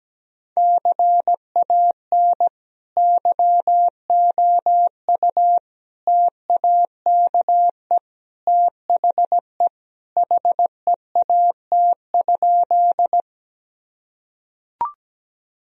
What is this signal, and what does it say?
Morse code, heard as a keyed tone, about 17 wpm, 700 Hz. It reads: CAN YOU TAKE THE HEAT?
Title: Sentences from Top 1000 Words - Review Part 3 - 17wpm